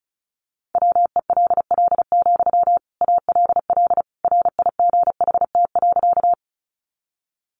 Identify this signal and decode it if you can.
Morse code, heard as a keyed tone, about 34 wpm, 700 Hz. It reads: WELL, ALL RIGHT.
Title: Sentences from Top 200 Words 35wpm